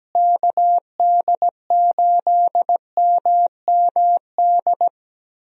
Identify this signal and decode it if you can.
Morse code, heard as a keyed tone, about 17 wpm, 700 Hz. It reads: KD8MMD
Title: USA Call Signs - 6 Characters - Repeated 3x - 17wpm